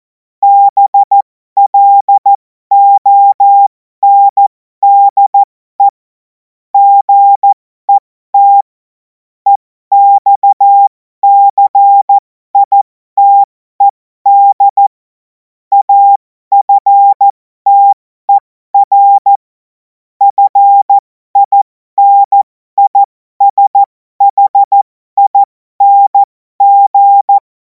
Morse code, heard as a keyed tone, about 14 wpm, spaced as for 12 wpm, 800 Hz. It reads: BLONDE GET EXCITED AFTER FINISHING
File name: Morsekurs / Witze